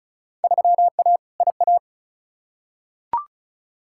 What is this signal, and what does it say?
Morse code, heard as a keyed tone, about 35 wpm, 700 Hz. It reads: 3A IA
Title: Field Day Exchanges Encore - 35wpm